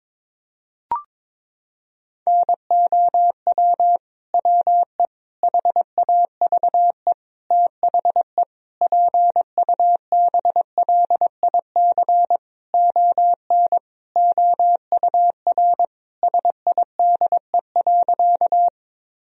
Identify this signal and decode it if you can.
Morse code, heard as a keyed tone, about 22 wpm, 700 Hz. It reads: NOW WE HAVE THE PUBLIC ON OUR SIDE.
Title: Sentences from Top 1000 Words - Review Part 3 - 22wpm